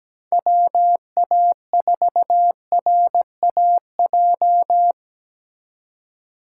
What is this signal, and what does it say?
Morse code, heard as a keyed tone, about 21 wpm, 700 Hz. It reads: WA4RAJ
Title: Speed Racing - USA Call Signs 17wpm